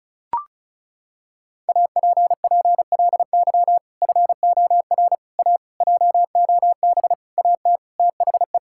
Morse code, heard as a keyed tone, about 36 wpm, 700 Hz. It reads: APPLY FOR A JOB AT THE
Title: Sentences from Top 1600 Words 35wpm